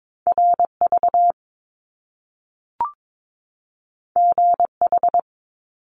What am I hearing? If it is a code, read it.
Morse code, heard as a keyed tone, about 22 wpm, 700 Hz. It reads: RV GH
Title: Top Two Letter Combinations in Words 22wpm